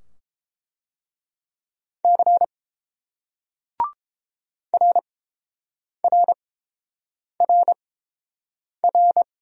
Morse code, heard as a keyed tone, about 30 wpm, spaced as for 7 wpm, 700 Hz. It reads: C RRRR